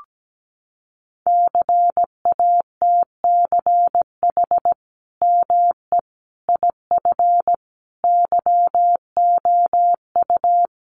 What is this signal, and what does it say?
Morse code, heard as a keyed tone, about 17 wpm, 700 Hz. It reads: CATCH ME IF YOU
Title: Sentences from Top 1500 Words 17wpm